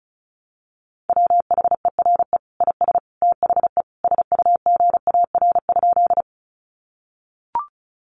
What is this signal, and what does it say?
Morse code, heard as a keyed tone, about 34 wpm, 700 Hz. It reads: WHERE IS THE SUGAR?
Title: Sentences from Top 1200 Words 35wpm